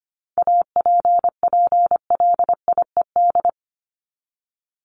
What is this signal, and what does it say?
Morse code, heard as a keyed tone, about 25 wpm, 700 Hz. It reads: APPLIED
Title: ED Suffix - Top 100 Words with Base Words 25wpm